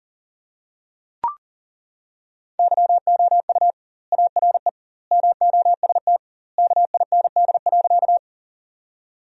Morse code, heard as a keyed tone, about 40 wpm, 700 Hz. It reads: YOU ARE MOST KIND.
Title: 2x Word-Spacing - Sentences from Top 200 Words - 40wpm